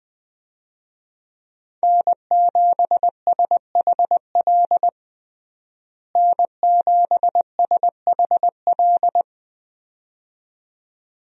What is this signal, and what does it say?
Morse code, heard as a keyed tone, about 20 wpm, 700 Hz. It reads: N7SHL N7SHL
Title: USA Call Signs - 5 Characters - Repeated 2x - 20wpm